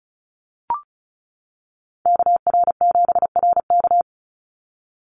Morse code, heard as a keyed tone, about 34 wpm, 700 Hz. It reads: KR7RK
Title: NA SOTA Activator Callsigns 35wpm